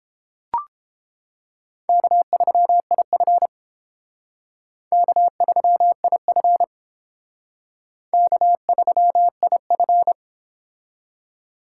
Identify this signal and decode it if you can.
Morse code, heard as a keyed tone, about 30 wpm, 700 Hz. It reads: K3IF K3IF K3IF